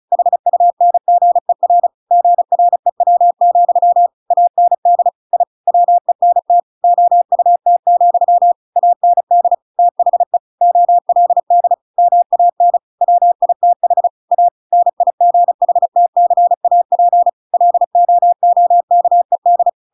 Morse code, heard as a keyed tone, about 34 wpm, 700 Hz. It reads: HUNGER GREW, AND I WENT OUT, AND THE OLD MAN WITH A NIGHTCAP LOOKED
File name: Pinocchio / Pinocchio ch7 35 35 ik1hge